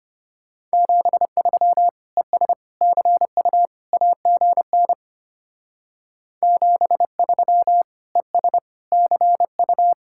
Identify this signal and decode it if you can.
Morse code, heard as a keyed tone, about 32 wpm, 700 Hz. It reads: 73 ES CU AGN 73 ES CU